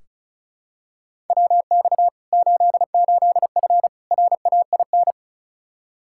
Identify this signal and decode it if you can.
Morse code, heard as a keyed tone, about 36 wpm, 700 Hz. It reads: WX 88F RAIN
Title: Practice QSO - Exchange 2 - Info - 35wpm